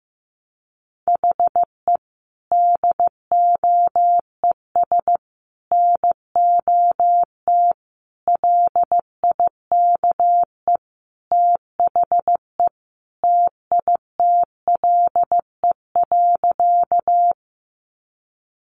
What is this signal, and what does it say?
Morse code, heard as a keyed tone, about 15 wpm, 700 Hz. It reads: HE DOES NOT LIKE THE TITLE.
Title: Sentences from Top 1200 Words 15wpm